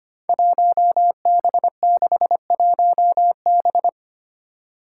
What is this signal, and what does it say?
Morse code, heard as a keyed tone, about 25 wpm, 700 Hz. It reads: 1B61B
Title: Five Challenging Letter-Number Combinations 25wpm